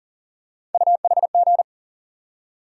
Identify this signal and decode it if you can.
Morse code, heard as a keyed tone, about 38 wpm, 700 Hz. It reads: UHG
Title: USA Call Sign Postfixes - 3 Letters 40wpm